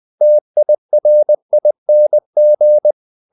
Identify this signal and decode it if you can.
Morse code, heard as a keyed tone, about 20 wpm, 600 Hz. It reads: TIRING
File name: the old man and the sea part 2 20wpm